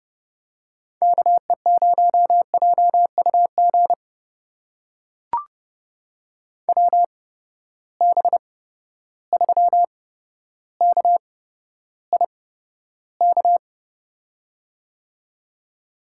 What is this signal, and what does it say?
Morse code, heard as a keyed tone, about 30 wpm, spaced as for 13 wpm, 700 Hz. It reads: KE0JUG WB3KIK
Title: USA Call Signs - 6 Characters - 8x Character Spacing - 30wpm